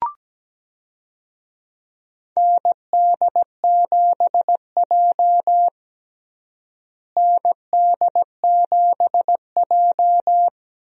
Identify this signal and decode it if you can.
Morse code, heard as a keyed tone, about 17 wpm, 700 Hz. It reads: ND7J ND7J
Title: USA Call Signs - 4 Characters - Repeated 3x - 17wpm